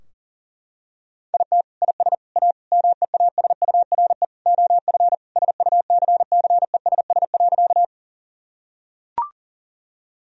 Morse code, heard as a keyed tone, about 37 wpm, 700 Hz. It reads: IT IS A MEASURE OF SUCCESS.
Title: Sentences from Top 1000 Words - Review Part 2 - 40wpm